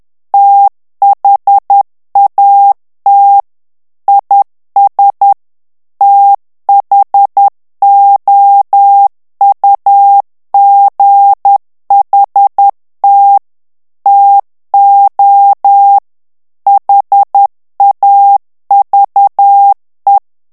Morse code, heard as a keyed tone, about 10 wpm, 800 Hz. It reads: THAT IS THOUGHT TO HAVE
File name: ANGLAIS texte clair 2 vitesse 10 mots minute